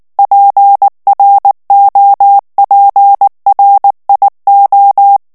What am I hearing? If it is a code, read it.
Morse code, heard as a keyed tone, about 19 wpm, 800 Hz. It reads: PROPRIO
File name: morse mp3 / ITALIEN texte clair 2 vitesse 18 mots minute